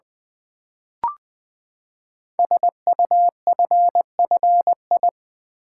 Morse code, heard as a keyed tone, about 20 wpm, 700 Hz. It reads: SUFFI
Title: Build A Word - 5 Letters At A Time - 10-Letter Words - 20wpm